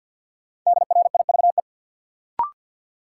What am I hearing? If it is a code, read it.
Morse code, heard as a keyed tone, about 53 wpm, 700 Hz. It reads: DRIVE
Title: Sentences from Top 500 Words - New Word Drill - 50wpm